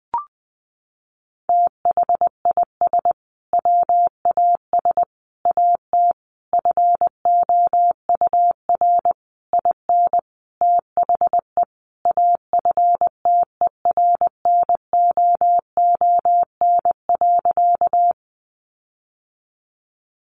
Morse code, heard as a keyed tone, about 19 wpm, 700 Hz. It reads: THIS WAS AT FOUR IN THE AFTERNOON.
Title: Sentences from Top 1500 Words 20wpm